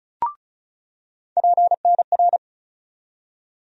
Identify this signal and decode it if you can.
Morse code, heard as a keyed tone, about 36 wpm, 700 Hz. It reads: PNR
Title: USA Call Sign Postfixes - 3 Letters 35wpm